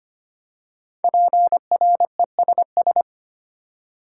Morse code, heard as a keyed tone, about 25 wpm, 700 Hz. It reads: PRESS
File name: 5 Letter Words - First 3 Letters Given - 25wpm